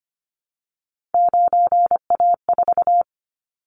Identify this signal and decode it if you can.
Morse code, heard as a keyed tone, about 25 wpm, 700 Hz. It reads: 9A4
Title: Callsign Prefixes for Top 25 DXCC - Repeated 3x - 25wpm